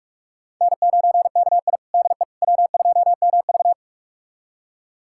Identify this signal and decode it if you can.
Morse code, heard as a keyed tone, about 42 wpm, 690 Hz. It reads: N9KI DE W2MV
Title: USA Callsign DE Callsign - 4 Characters - 45wpm